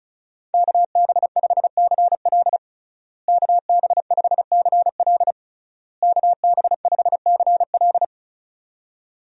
Morse code, heard as a keyed tone, about 36 wpm, 700 Hz. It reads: KB5CL KB5CL KB5CL
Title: USA Call Signs - 5 Characters - Repeated 3x - 35wpm